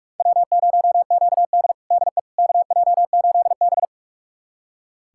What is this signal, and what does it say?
Morse code, heard as a keyed tone, about 47 wpm, 700 Hz. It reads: W0QD DE KJ8B
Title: USA Callsign DE Callsign - 4 Characters - 45wpm